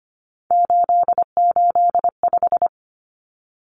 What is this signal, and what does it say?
Morse code, heard as a keyed tone, about 25 wpm, 700 Hz. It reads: 885
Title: Numbers - 3-Digit 25wpm